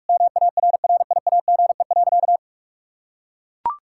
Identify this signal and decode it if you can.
Morse code, heard as a keyed tone, about 41 wpm, 700 Hz. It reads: MARRIAGE.
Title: Sentences from Top 1000 Words - Review Part 6 - 45wpm